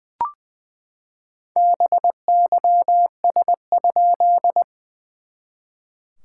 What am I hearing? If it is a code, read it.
Morse code, heard as a keyed tone, about 20 wpm, 690 Hz. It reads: BYS?
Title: SOTA QSO Practice 20wpm